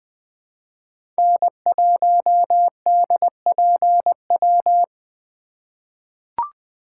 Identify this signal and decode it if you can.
Morse code, heard as a keyed tone, about 20 wpm, 700 Hz. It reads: N1DPW
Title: USA Call Signs - 8x Character Spacing - 20wpm